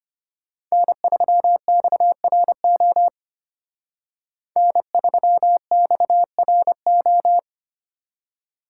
Morse code, heard as a keyed tone, about 32 wpm, 700 Hz. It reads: N3XRO N3XRO